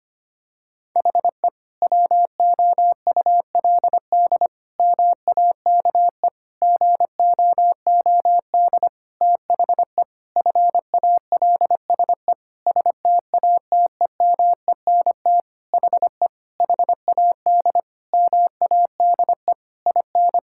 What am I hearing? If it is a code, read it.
Morse code, heard as a keyed tone, about 25 wpm, 700 Hz. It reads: HE WOULD MAKE GOOD THE FALSE STATEMENT HE HAD MADE IN
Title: Sentences from Top 1600 Words 25wpm